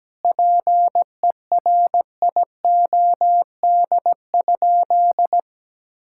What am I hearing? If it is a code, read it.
Morse code, heard as a keyed tone, about 17 wpm, 700 Hz. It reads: PERIOD?
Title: Sentences from Top 1000 Words - Review Part 5 - 17wpm